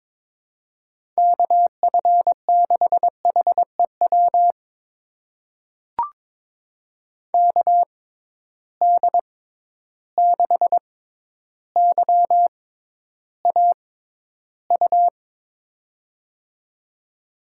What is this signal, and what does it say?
Morse code, heard as a keyed tone, about 22 wpm, spaced as for 11 wpm, 700 Hz. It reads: KF6HEW KD6YAU
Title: USA Call Signs - 6 Characters - 6x Character Spacing - 22wpm